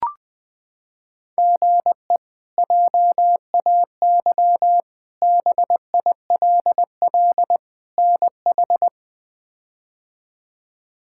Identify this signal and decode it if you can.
Morse code, heard as a keyed tone, about 21 wpm, 700 Hz. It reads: GE JAY BILL NH
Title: K1USN SST QSO Practice - 20wpm